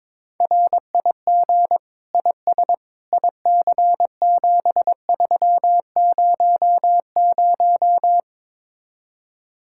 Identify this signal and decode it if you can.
Morse code, heard as a keyed tone, about 22 wpm, 700 Hz. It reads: RIG IS IC7300
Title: Speed-Racing Common QSO Phrases Encore - 22wpm